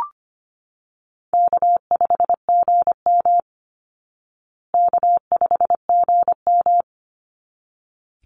Morse code, heard as a keyed tone, about 25 wpm, 700 Hz. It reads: K5GM K5GM
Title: Call Signs Repeated 3x - 25wpm